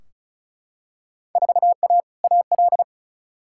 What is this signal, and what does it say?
Morse code, heard as a keyed tone, about 36 wpm, 700 Hz. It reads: 4A AL
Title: Field Day Exchanges - 35wpm